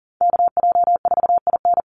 Morse code, heard as a keyed tone, about 40 wpm, 700 Hz. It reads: KJ4IN
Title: USA Call Signs 40wpm